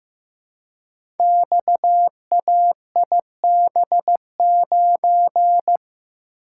Morse code, heard as a keyed tone, about 15 wpm, 700 Hz. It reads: XAIB9
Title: Code Groups - 5 Letter:Numbers - 15wpm